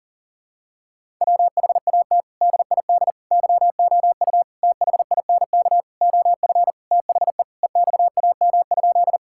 Morse code, heard as a keyed tone, about 40 wpm, 700 Hz. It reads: WHAT DID YOU THINK OF THE EXAM?
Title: Sentences from Top 1000 Words - Review Part 5 - 40wpm